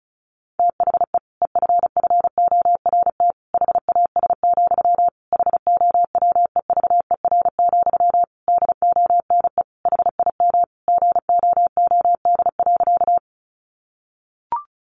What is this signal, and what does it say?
Morse code, heard as a keyed tone, about 34 wpm, 700 Hz. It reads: THE EFFORT HAS, HOWEVER, DONE HIM GOOD.
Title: Sentences from Top 1200 Words 35wpm